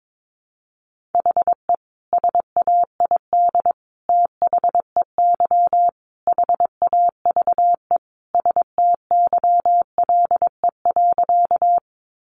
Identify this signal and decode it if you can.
Morse code, heard as a keyed tone, about 22 wpm, 700 Hz. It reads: HE SAID THEY HAVE STYLE.
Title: Sentences from Top 800 Words 22wpm